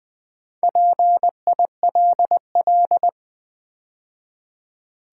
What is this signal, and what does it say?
Morse code, heard as a keyed tone, about 20 wpm, 700 Hz. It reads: PILL
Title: Build A Word - 2 Letters At A Time - 4-Letter Words - 20wpm